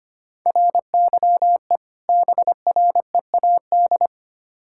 Morse code, heard as a keyed tone, about 25 wpm, 700 Hz. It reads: RYE BREAD
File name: practice / 25wpm / Difficult Words with Hints - 25wpm